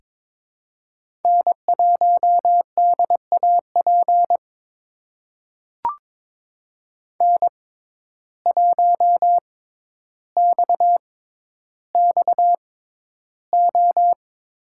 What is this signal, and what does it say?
Morse code, heard as a keyed tone, about 22 wpm, spaced as for 11 wpm, 700 Hz. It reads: N1DAP N1XXO
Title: USA Call Signs - 5 Characters - 6x Character Spacing - 22wpm